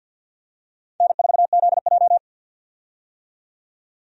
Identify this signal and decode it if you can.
Morse code, heard as a keyed tone, about 51 wpm, 700 Hz. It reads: N4ZJ
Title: USA Call Signs - 4 Characters - 50wpm